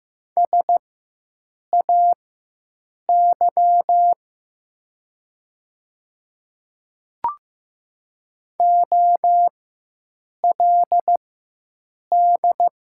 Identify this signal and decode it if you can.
Morse code, heard as a keyed tone, about 15 wpm, spaced as for 7 wpm, 700 Hz. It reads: SAY OLD
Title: 4x Character Spacing - Top 200 Words - 15wpm